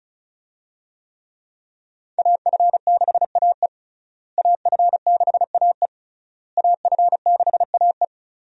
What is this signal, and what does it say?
Morse code, heard as a keyed tone, about 36 wpm, 700 Hz. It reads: AF6AE AF6AE AF6AE
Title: USA Call Signs - 5 Characters - Repeated 3x - 35wpm